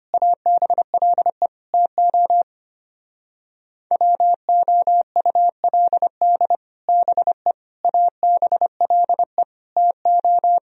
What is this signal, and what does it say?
Morse code, heard as a keyed tone, about 31 wpm, 700 Hz. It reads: ABLE TO WOULD BE ABLE TO
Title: Speed-Racing - Sets of 4 Words 25wpm